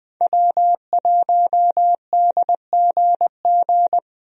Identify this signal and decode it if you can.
Morse code, heard as a keyed tone, about 20 wpm, 700 Hz. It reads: W1DGG
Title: USA Call Signs - 5 Characters - Repeated 3x - 20wpm